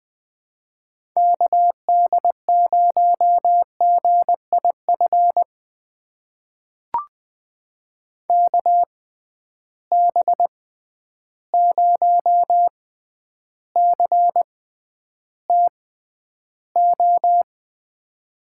Morse code, heard as a keyed tone, about 20 wpm, spaced as for 10 wpm, 700 Hz. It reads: KD0GIF KB0CTO
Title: USA Call Signs - 6 Characters - 6x Character Spacing - 20wpm